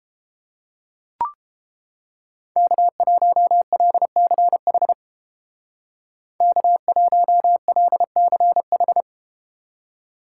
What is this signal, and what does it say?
Morse code, heard as a keyed tone, about 30 wpm, 700 Hz. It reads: K1LCH K1LCH